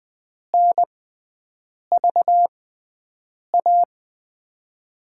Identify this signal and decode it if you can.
Morse code, heard as a keyed tone, about 20 wpm, spaced as for 7 wpm, 700 Hz. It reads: NVA
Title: USA Call Signs - 5 Characters - 6x Character Spacing - 20wpm